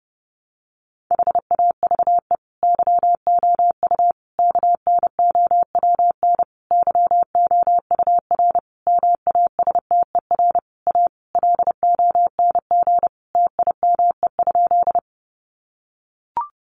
Morse code, heard as a keyed tone, about 30 wpm, 700 Hz. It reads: HAVE YOU KNOWN YOUR MASTER A LONG TIME?
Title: Sentences from Top 1300 Words 30wpm